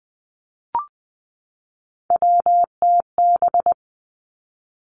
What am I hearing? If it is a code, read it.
Morse code, heard as a keyed tone, about 20 wpm, 700 Hz. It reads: WTB
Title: Three Challenging Letter-Number Combinations 20wpm